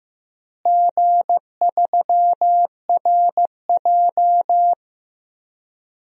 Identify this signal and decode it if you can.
Morse code, heard as a keyed tone, about 15 wpm, 700 Hz. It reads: G3RJ
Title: DXCC-100 Call Signs - 4 Characters - Repeated 2x - 15wpm